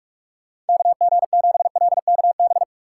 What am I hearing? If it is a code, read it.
Morse code, heard as a keyed tone, about 59 wpm, spaced as for 52 wpm, 700 Hz. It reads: KG7LKB
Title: Speed Racing - USA 2x3 Call Signs 45wpm